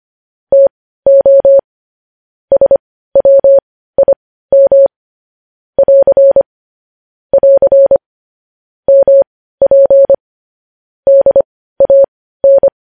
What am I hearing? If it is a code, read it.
Morse code, heard as a keyed tone, about 25 wpm, spaced as for 15 wpm, 550 Hz. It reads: TO SWIM <AR> <AR> MP DAN
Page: news-headlines-in-morse-code-at-15-wpm